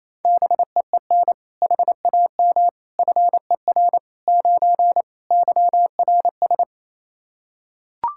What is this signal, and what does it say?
Morse code, heard as a keyed tone, about 28 wpm, 700 Hz. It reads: BEEN HAM FER 9 YRS
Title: Practice QSO - Exchange 3 - More Info - 28wpm